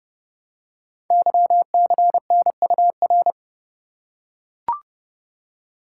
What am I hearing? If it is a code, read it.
Morse code, heard as a keyed tone, about 30 wpm, 700 Hz. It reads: YCNUR
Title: Code Groups - 5 Letter:Numbers - 30wpm